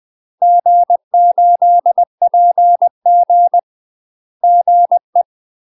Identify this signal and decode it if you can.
Morse code, heard as a keyed tone, about 20 wpm, 700 Hz. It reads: G8PG GE
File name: UK MASTER QSOs TRAINING/20WPM